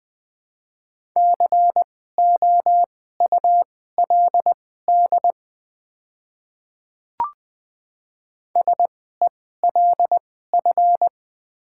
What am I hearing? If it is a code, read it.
Morse code, heard as a keyed tone, about 20 wpm, spaced as for 15 wpm, 700 Hz. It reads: COULD SELF
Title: 2x Character Spacing - Top 300 Words - 20wpm